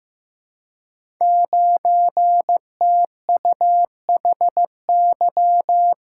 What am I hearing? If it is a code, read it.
Morse code, heard as a keyed tone, about 15 wpm, 700 Hz. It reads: 9TUHY
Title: Code Groups - 5 Letter:Numbers - 15wpm